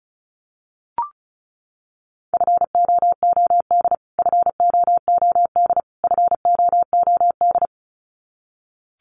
Morse code, heard as a keyed tone, about 34 wpm, 700 Hz. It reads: FOOD FOOD FOOD
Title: Top 500 Words Repeated 35wpm